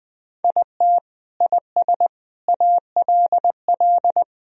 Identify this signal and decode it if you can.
Morse code, heard as a keyed tone, about 19 wpm, 700 Hz. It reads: IT IS ALL
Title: Sentences from Top 1000 Words - Review Part 4 - 20wpm